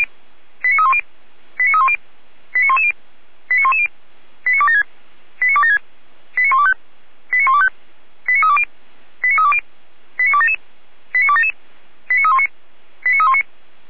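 Five-tone selective call (ZVEI), heard as quick runs of five tones, tones of about 70 ms. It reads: 89310 89310 89110 89110 89276 89276 89125 89125 89320 89320 89277 89277 89319 89319
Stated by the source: ZVEI five-tone selective call received with an SDR.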